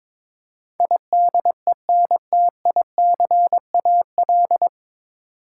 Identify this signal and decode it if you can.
Morse code, heard as a keyed tone, about 23 wpm, 700 Hz. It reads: IDENTICAL
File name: 9 Letter Words - First 3 Letters Given - 22wpm